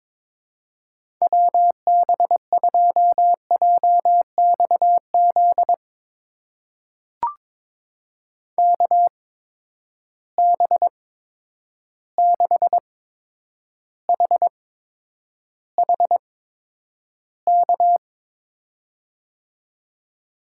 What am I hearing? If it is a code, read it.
Morse code, heard as a keyed tone, about 22 wpm, spaced as for 9 wpm, 700 Hz. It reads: WB2JXZ KB6HHK
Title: USA Call Signs - 6 Characters - 8x Character Spacing - 22wpm